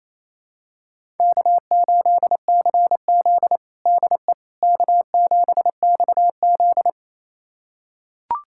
Morse code, heard as a keyed tone, about 27 wpm, 700 Hz. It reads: K8CZ DE K7XZ
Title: USA Callsign DE Callsign - 4 Characters - 28wpm